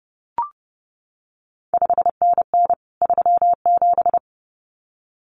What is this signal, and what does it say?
Morse code, heard as a keyed tone, about 30 wpm, 700 Hz. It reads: 5NN 37
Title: CQ WW DX Contest Exchange Encore - 30wpm